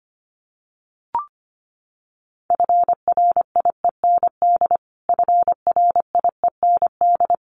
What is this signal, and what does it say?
Morse code, heard as a keyed tone, about 25 wpm, 700 Hz. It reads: FRIEND FRIEND
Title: Top 500 Words Repeated 25wpm